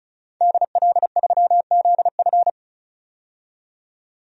Morse code, heard as a keyed tone, about 34 wpm, 700 Hz. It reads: DL3ZF
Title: DXCC-100 Call Signs - 5 Characters - 35wpm